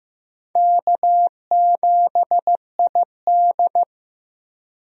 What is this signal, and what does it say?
Morse code, heard as a keyed tone, about 16 wpm, 700 Hz. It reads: K7ID
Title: USA Call Signs - 4 Characters - Repeated 2x - 15wpm